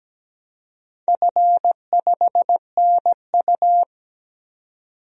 Morse code, heard as a keyed tone, about 17 wpm, 700 Hz. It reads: F5NU
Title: DXCC-100 Call Signs - 4 Characters - 17wpm